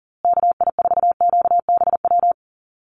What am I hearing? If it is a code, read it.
Morse code, heard as a keyed tone, about 40 wpm, 700 Hz. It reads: KI4QBW
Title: USA Call Signs Repeated 2x 40wpm